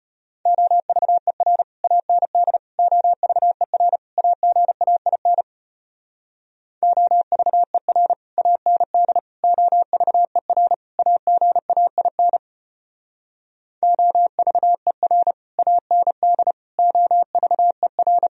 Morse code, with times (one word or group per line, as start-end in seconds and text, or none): 0.5-1.6 OVER
1.8-2.6 AND
2.8-4.0 OVER
4.2-5.4 AGAIN
6.8-8.1 OVER
8.4-9.2 AND
9.4-10.7 OVER
11.0-12.4 AGAIN
13.8-15.3 OVER
15.6-16.5 AND
16.8-18.3 OVER